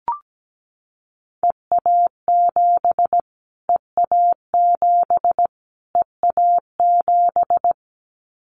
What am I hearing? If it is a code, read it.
Morse code, heard as a keyed tone, about 17 wpm, 700 Hz. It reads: EA7 EA7 EA7
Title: Callsign Prefixes for Top 25 DXCC - Repeated 3x - 17wpm